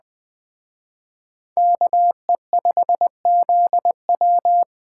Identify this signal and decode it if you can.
Morse code, heard as a keyed tone, about 20 wpm, 700 Hz. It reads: KE5ZW
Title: Speed Racing - USA 2x2 Call Signs 17wpm